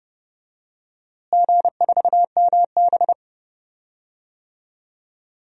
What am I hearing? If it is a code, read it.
Morse code, heard as a keyed tone, about 30 wpm, 700 Hz. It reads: G4MB
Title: DXCC-100 Call Signs - 4 Characters - Repeated 2x - 30wpm